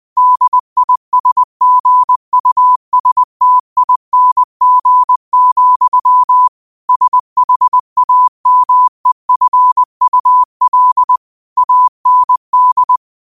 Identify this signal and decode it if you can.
Morse code, heard as a keyed tone, about 20 wpm, 1000 Hz. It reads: DISGUSTING, SHAMEFUL AND